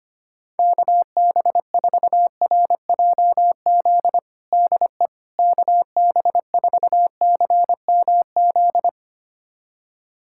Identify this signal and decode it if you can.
Morse code, heard as a keyed tone, about 25 wpm, 700 Hz. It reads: KB4RJZ DE KB4CMZ
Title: USA Callsign DE Callsign - 6 Characters - 25wpm